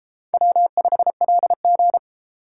Morse code, heard as a keyed tone, about 30 wpm, 700 Hz. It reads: W5LG